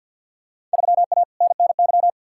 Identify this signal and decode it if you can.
Morse code, heard as a keyed tone, about 51 wpm, 700 Hz. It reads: 3A NNY
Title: Field Day Exchanges - 50wpm